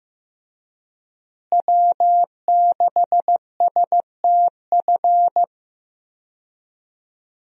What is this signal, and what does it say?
Morse code, heard as a keyed tone, about 15 wpm, 700 Hz. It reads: W6STF
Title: USA Call Signs - 5 Characters - Repeated 2x - 15wpm